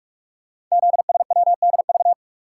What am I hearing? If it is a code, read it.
Morse code, heard as a keyed tone, about 44 wpm, 700 Hz. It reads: ZSWDV
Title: Code Groups - 5 Letters - 45wpm